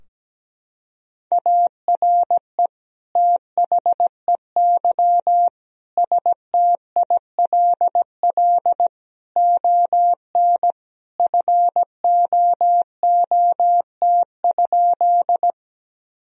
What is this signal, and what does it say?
Morse code, heard as a keyed tone, about 17 wpm, 700 Hz. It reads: ARE THEY STILL ON FOOT?
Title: Sentences from Top 1000 Words - Review Part 3 - 17wpm